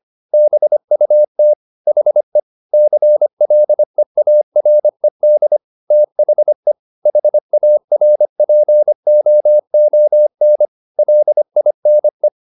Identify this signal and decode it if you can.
Morse code, heard as a keyed tone, about 25 wpm, 600 Hz. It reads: BUT HE CLEARED THE HARPOON LINE